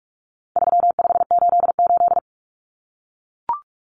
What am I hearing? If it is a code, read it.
Morse code, heard as a keyed tone, about 44 wpm, 700 Hz. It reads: 3588
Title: Common Cut Numbers - 4-Digit 45wpm